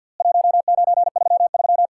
Morse code, heard as a keyed tone, about 50 wpm, 700 Hz. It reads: 1933
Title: Worked All Europe DX Contest - QTC Exchange with Repeats - 50wpm